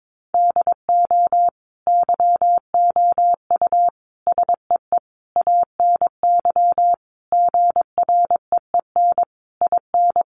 Morse code, heard as a keyed tone, about 22 wpm, 700 Hz. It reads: DO YOU SEE ANY GREEN IN
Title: Sentences from Top 500 Words 22wpm